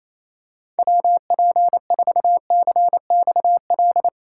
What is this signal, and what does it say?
Morse code, heard as a keyed tone, about 27 wpm, 700 Hz. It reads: WP4CXL
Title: USA Call Signs - 6 Characters - 8x Character Spacing - 28wpm